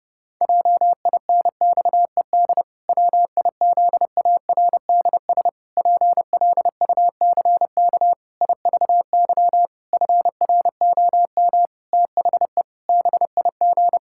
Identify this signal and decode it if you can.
Morse code, heard as a keyed tone, about 30 wpm, 700 Hz. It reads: JINXED WIZARDS PLUCK IVY FROM THE BIG
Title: Pangram Challenge v2 - 30wpm